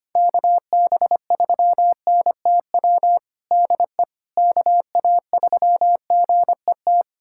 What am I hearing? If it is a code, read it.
Morse code, heard as a keyed tone, about 25 wpm, 700 Hz. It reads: KB3NTW DE KA3GET
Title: USA Callsign DE Callsign - 6 Characters - 25wpm